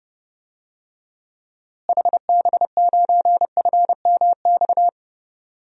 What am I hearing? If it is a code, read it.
Morse code, heard as a keyed tone, about 30 wpm, 700 Hz. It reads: HB9FMX
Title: DXCC-100 Call Signs - 6 Characters - Repeated 2x - 30wpm